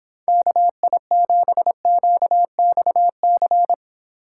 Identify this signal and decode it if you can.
Morse code, heard as a keyed tone, about 26 wpm, 700 Hz. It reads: KI7QXC